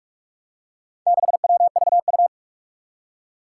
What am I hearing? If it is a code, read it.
Morse code, heard as a keyed tone, about 46 wpm, 700 Hz. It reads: 6WVU
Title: Four Challenging Letter-Number Combinations 45wpm